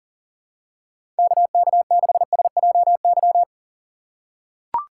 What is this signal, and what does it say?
Morse code, heard as a keyed tone, about 40 wpm, 700 Hz. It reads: KK6SJY
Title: USA Call Signs - 6 Characters - 4x Character Spacing - 40wpm